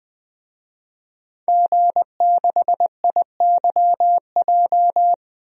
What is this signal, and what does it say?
Morse code, heard as a keyed tone, about 20 wpm, 700 Hz. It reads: G6IYJ
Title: DXCC-100 Call Signs - 5 Characters - Repeated 3x - 20wpm